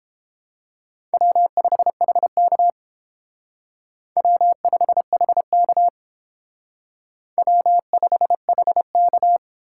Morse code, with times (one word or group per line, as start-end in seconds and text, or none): 1.1-2.7 W5HK
4.2-5.9 W5HK
7.4-9.4 W5HK